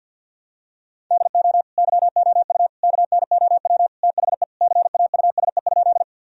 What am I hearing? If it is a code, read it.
Morse code, heard as a keyed tone, about 52 wpm, 700 Hz. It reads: DO YOU KNOW THE CAUSE?
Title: Sentences from Top 1000 Words - Review Part 1 - 50wpm